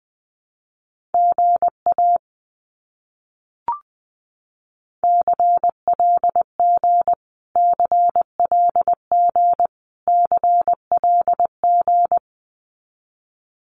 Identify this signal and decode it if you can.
Morse code, heard as a keyed tone, about 20 wpm, 700 Hz. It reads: GA CLG CLG CLG
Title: Abbreviations Repeated 3x - 20wpm